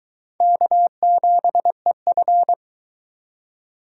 Morse code, heard as a keyed tone, about 22 wpm, 700 Hz. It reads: K7EF